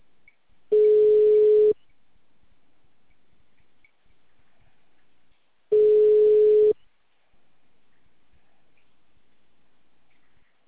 Ringback tone (Europe, 425 Hz).